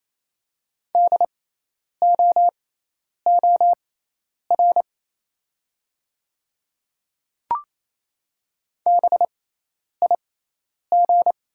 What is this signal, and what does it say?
Morse code, heard as a keyed tone, about 28 wpm, spaced as for 10 wpm, 690 Hz. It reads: DOOR BIG
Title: 6x Character Spacing - Top 300 Words - 28wpm